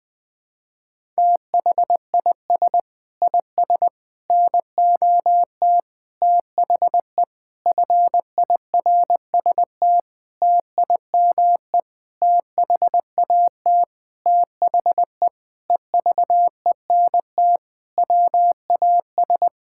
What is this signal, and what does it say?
Morse code, heard as a keyed tone, about 19 wpm, 700 Hz. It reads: THIS IS NOT THE FIRST TIME THAT THE EVENT WAS
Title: Sentences with 8-Word Phrases - 20wpm